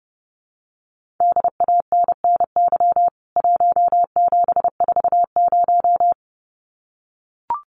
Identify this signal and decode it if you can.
Morse code, heard as a keyed tone, about 30 wpm, 700 Hz. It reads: DANNY 1740
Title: CWT Exchange 30wpm